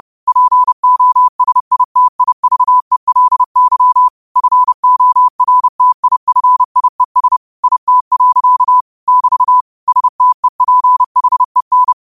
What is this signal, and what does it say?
Morse code, heard as a keyed tone, about 30 wpm, 1000 Hz. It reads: POSITIVELY FORTIFIES IT. <BT> STEPHEN